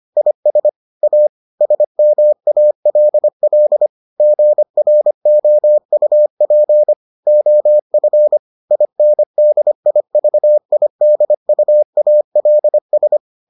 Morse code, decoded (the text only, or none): IS A SMALL GROUP OF INDIVIDUALS